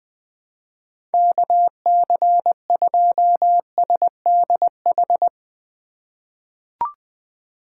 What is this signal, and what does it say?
Morse code, heard as a keyed tone, about 20 wpm, 700 Hz. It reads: KC2SDH
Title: USA Call Signs - 6 Characters - 2x Character Spacing - 20wpm